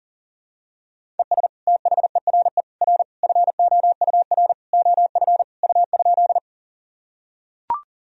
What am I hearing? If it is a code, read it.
Morse code, heard as a keyed tone, about 40 wpm, 700 Hz. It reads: ES THERE R FOUR OF U?
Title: Sentences from Top 300 Words and Abbreviations 40wpm